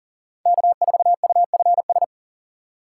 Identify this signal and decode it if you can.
Morse code, heard as a keyed tone, about 52 wpm, 700 Hz. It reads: K4UFS